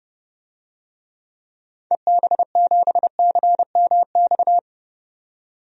Morse code, heard as a keyed tone, about 30 wpm, 700 Hz. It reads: EB7CMX
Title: DXCC-100 Call Signs - 6 Characters - 30wpm